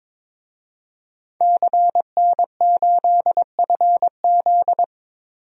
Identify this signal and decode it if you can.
Morse code, heard as a keyed tone, about 22 wpm, 700 Hz. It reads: CN8FZ
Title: DXCC-100 Call Signs - 5 Characters - 22wpm